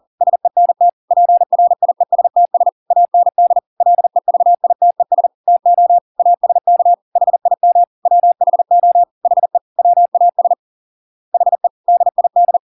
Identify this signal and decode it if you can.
Morse code, heard as a keyed tone, about 40 wpm, 700 Hz. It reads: SENT PRIESTS AND LEVITES TO ASK HIM WHO HE WAS HE DID